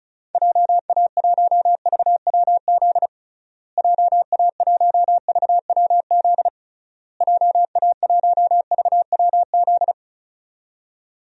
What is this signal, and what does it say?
Morse code, heard as a keyed tone, about 36 wpm, 700 Hz. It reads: JA1VWZ JA1VWZ JA1VWZ
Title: DXCC-100 Call Signs - 6 Characters - Repeated 3x - 35wpm